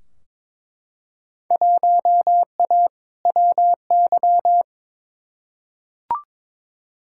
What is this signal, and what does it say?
Morse code, heard as a keyed tone, about 22 wpm, 700 Hz. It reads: 1A WY
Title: Field Day Exchanges Encore - 22wpm